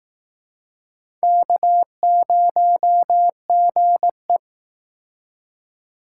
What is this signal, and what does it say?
Morse code, heard as a keyed tone, about 19 wpm, 700 Hz. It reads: K0GE